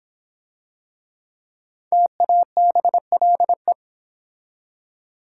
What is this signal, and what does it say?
Morse code, heard as a keyed tone, about 26 wpm, 700 Hz. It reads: TABLE